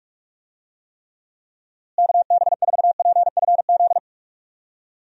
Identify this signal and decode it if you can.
Morse code, heard as a keyed tone, about 41 wpm, 700 Hz. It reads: KB4PFZ
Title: USA Call Signs - 6 Characters - Repeated 2x - 45wpm